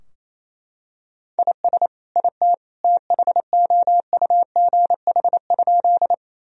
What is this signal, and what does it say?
Morse code, heard as a keyed tone, about 28 wpm, 700 Hz. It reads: IS IT THOUGH?
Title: Sentences from Top 1000 Words - Review Part 2 - 28wpm